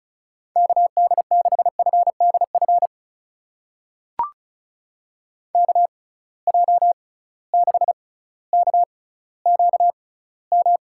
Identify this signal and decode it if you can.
Morse code, heard as a keyed tone, about 36 wpm, spaced as for 18 wpm, 700 Hz. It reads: KD6FDF KJ6KQM
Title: USA Call Signs - 6 Characters - 6x Character Spacing - 35wpm